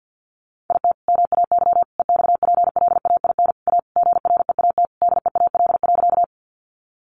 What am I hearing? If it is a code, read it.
Morse code, heard as a keyed tone, about 50 wpm, 700 Hz. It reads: IT MAY EXPLAIN A GREAT DEAL.
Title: Sentences from Top 1100 Words 50wpm